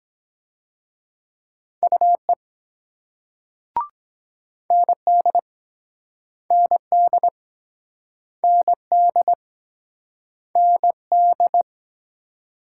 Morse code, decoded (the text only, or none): UE ND ND ND ND